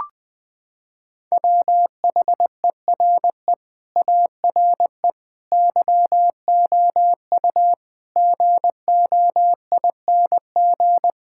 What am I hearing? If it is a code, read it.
Morse code, heard as a keyed tone, about 20 wpm, 700 Hz. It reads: WHERE ARE YOU GOING
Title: Sentences from Top 1600 Words 20wpm